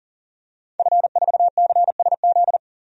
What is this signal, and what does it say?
Morse code, heard as a keyed tone, about 40 wpm, 700 Hz. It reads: F4CSZ
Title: DXCC-100 Call Signs - 5 Characters - 40wpm